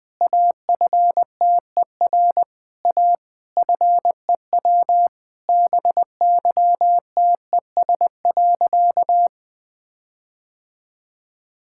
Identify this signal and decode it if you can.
Morse code, heard as a keyed tone, about 20 wpm, 700 Hz. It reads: AFTER A FEW BYTES.
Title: All-Age Appropriate Jokes - 20wpm